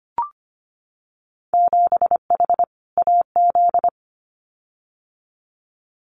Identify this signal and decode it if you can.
Morse code, heard as a keyed tone, about 25 wpm, 700 Hz. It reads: 7H AZ
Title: Winter Field Day Exchanges Encore 25wpm